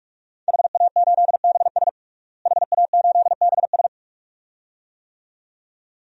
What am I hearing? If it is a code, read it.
Morse code, heard as a keyed tone, about 42 wpm, 700 Hz. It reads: HA8BS HA8BS
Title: DXCC-100 Call Signs - 5 Characters - Repeated 2x - 45wpm